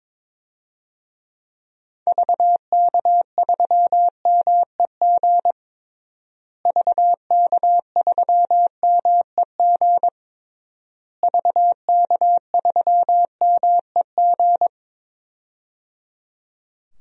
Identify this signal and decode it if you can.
Morse code, heard as a keyed tone, about 21 wpm, 700 Hz. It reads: VK3MEG VK3MEG VK3MEG
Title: DXCC-100 Call Signs - 6 Characters - Repeated 3x - 22wpm